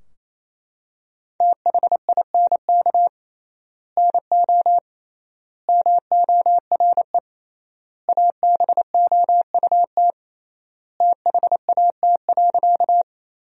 Morse code, heard as a keyed tone, about 28 wpm, 700 Hz. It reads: THINK NO MORE ABOUT THAT.
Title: 3x Word-Spacing - Sentences from Top 200 Words - 28wpm